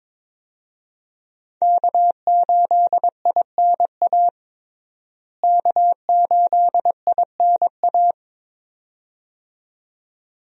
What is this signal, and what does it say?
Morse code, heard as a keyed tone, about 22 wpm, 700 Hz. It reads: K8INA K8INA
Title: USA Call Signs - 5 Characters - Repeated 2x - 22wpm